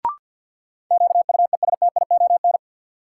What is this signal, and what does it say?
Morse code, heard as a keyed tone, about 53 wpm, 700 Hz. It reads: QUESTION
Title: Sentences from Top 400 Words - New Word Drill - 50wpm